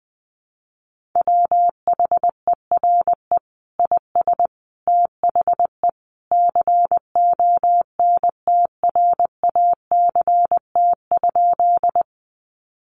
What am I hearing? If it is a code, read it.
Morse code, heard as a keyed tone, about 19 wpm, 700 Hz. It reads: WHERE IS THE CONTRACT?
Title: Sentences from Top 1400 Words 20wpm